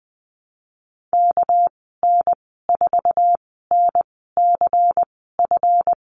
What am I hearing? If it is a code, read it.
Morse code, heard as a keyed tone, about 20 wpm, spaced as for 15 wpm, 700 Hz. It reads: KN4NCF
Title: 213 Lesson 34g - Number 8 - Call Signs 20wpm